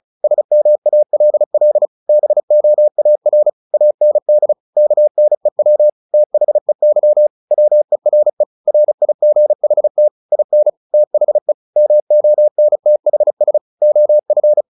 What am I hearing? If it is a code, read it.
Morse code, heard as a keyed tone, about 34 wpm, 600 Hz. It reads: SMALL BOAR AND KNEW THEY WERE RIGHT IN THE MONTHS OF
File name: the old man and the sea / the old man and the sea 1 35wpm